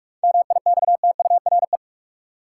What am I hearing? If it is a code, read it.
Morse code, heard as a keyed tone, about 42 wpm, 700 Hz. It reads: MIXTURE
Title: Seven Letter Words - 45wpm